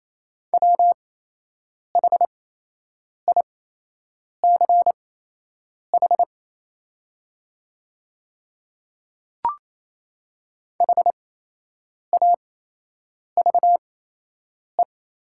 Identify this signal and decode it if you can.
Morse code, heard as a keyed tone, about 28 wpm, spaced as for 8 wpm, 690 Hz. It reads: WHICH HAVE
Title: 8x Character Spacing - Top 100 Words - 28wpm